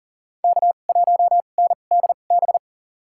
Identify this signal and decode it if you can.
Morse code, heard as a keyed tone, about 40 wpm, spaced as for 29 wpm, 700 Hz. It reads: K1NDB
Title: USA Call Signs - 5 Characters - 2x Character Spacing - 40wpm